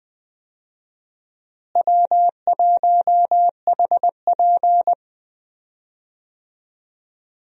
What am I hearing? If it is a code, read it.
Morse code, heard as a keyed tone, about 20 wpm, 700 Hz. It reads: W1HP
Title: USA Call Signs - 4 Characters - Repeated 3x - 20wpm